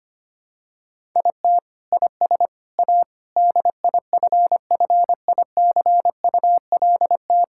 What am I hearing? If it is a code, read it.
Morse code, heard as a keyed tone, about 25 wpm, 700 Hz. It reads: IT IS A DIFFICULT
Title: Sentences from Top 1000 Words - Review Part 6 - 25wpm